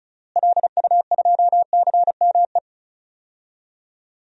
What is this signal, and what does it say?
Morse code, heard as a keyed tone, about 34 wpm, 700 Hz. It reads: LU2CME
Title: DXCC-100 Call Signs - 6 Characters - Repeated 3x - 35wpm